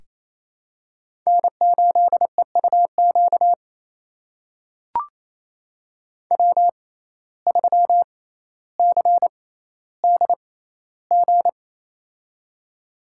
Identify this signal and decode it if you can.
Morse code, heard as a keyed tone, about 28 wpm, spaced as for 14 wpm, 700 Hz. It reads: N8EUQ W3CDG
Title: USA Call Signs - 5 Characters - 6x Character Spacing - 28wpm